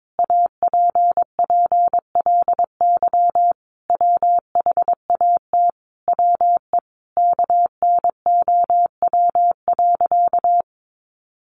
Morse code, heard as a keyed tone, about 22 wpm, 700 Hz. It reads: APPLY WHAT WE KNOW.
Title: Sentences from Top 1200 Words 22wpm